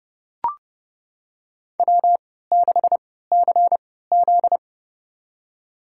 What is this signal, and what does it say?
Morse code, heard as a keyed tone, about 29 wpm, spaced as for 17 wpm, 700 Hz. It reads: W6CZ
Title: USA Call Signs - 4 Characters - 3x Character Spacing - 30wpm